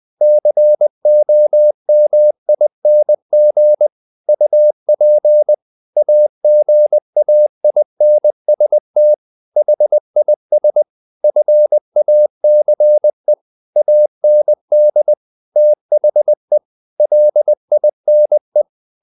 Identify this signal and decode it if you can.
Morse code, heard as a keyed tone, about 20 wpm, 600 Hz. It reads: COMING UP AGAINST HIS FACE AND THE LINE
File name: the old man and the sea part 2 20wpm